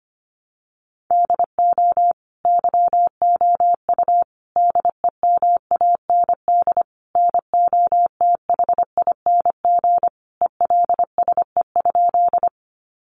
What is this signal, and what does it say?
Morse code, heard as a keyed tone, about 25 wpm, 700 Hz. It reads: DO YOU DEMAND NOTHING ELSE?
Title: Sentences from Top 800 Words 25wpm